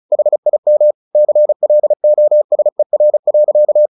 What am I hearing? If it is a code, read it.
Morse code, heard as a keyed tone, about 34 wpm, 600 Hz. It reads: HIM CLOSER.
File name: the old man and the sea 2 35wpm